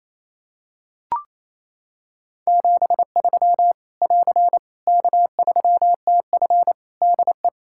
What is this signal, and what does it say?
Morse code, heard as a keyed tone, about 28 wpm, 700 Hz. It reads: 73 <AR> K3TF DE
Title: Practice QSO - Exchange 4 - Ending - 28wpm